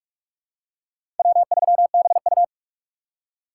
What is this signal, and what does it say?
Morse code, heard as a keyed tone, about 46 wpm, 700 Hz. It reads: W3BU
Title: USA Call Signs - 4 Characters - 4x Character Spacing - 45wpm